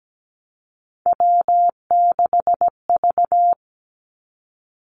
Morse code, heard as a keyed tone, about 17 wpm, 700 Hz. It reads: W6V
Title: Three Challenging Letter-Number Combinations 17wpm